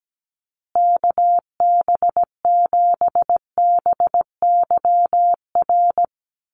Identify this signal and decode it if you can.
Morse code, heard as a keyed tone, about 17 wpm, 700 Hz. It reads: KB7BYR
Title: USA Call Signs Repeated 2x 17wpm